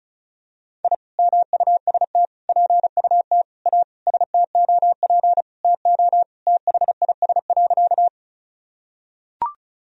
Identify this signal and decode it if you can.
Morse code, heard as a keyed tone, about 36 wpm, 700 Hz. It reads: I MUST PUT A STOP TO THIS.
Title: Sentences from Top 1000 Words - Review Part 1 - 35wpm